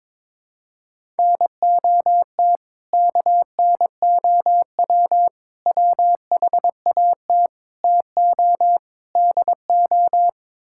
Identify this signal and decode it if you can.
Morse code, heard as a keyed tone, about 22 wpm, 700 Hz. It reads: NOT KNOW WHAT TO DO
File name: practice / 22wpm / Build A Phrase - 5-Word Phrases - 22wpm